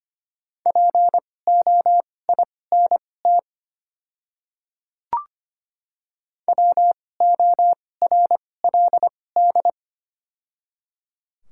Morse code, heard as a keyed tone, about 25 wpm, spaced as for 18 wpm, 690 Hz. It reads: POINT WORLD
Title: 2x Character Spacing - Top 200 Words - 25wpm